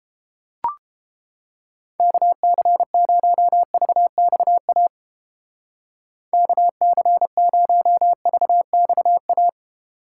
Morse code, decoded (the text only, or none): KC0VXA KC0VXA